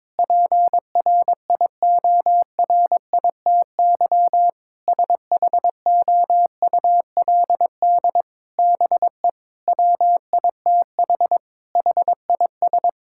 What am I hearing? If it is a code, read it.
Morse code, heard as a keyed tone, about 22 wpm, 700 Hz. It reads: PRIORITY SHOULD BE WITH HIS
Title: Sentences from Top 1600 Words 22wpm